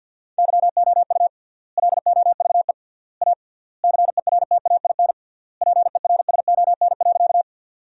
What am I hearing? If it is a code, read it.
Morse code, heard as a keyed tone, about 50 wpm, 700 Hz. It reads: YOU LOVE A CERTAIN PERSON.
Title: 3x Word-Spacing - Sentences from Top 500 Words - 50wpm